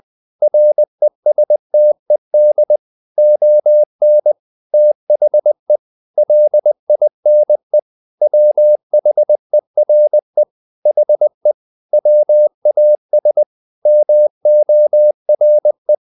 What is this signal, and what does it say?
Morse code, heard as a keyed tone, about 20 wpm, 600 Hz. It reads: RESTED ON THE LINE WHERE HE WAS MORE
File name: the old man and the sea / the old man and the sea part 1 20wpm